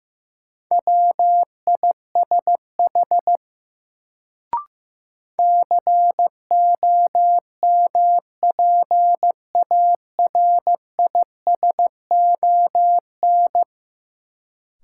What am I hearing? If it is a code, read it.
Morse code, heard as a keyed tone, about 15 wpm, 700 Hz. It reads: WISH COMPARISON
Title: Sentences from Top 1600 Words - New Word Drill Alt - 15wpm